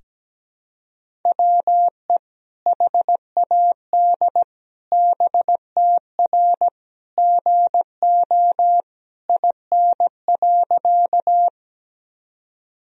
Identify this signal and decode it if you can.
Morse code, heard as a keyed tone, about 17 wpm, 700 Hz. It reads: WE HAD BTR GO IN.
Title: Sentences from Top 400 Words and Abbreviations 17wpm